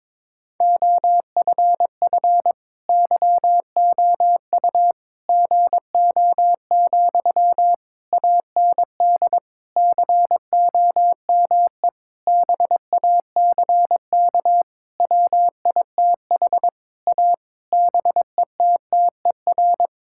Morse code, heard as a keyed tone, about 22 wpm, 700 Hz. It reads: OFF YOU GO, AND COME BACK WITH A BETTER
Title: Sentences from Top 1000 Words - Review Part 6 - 22wpm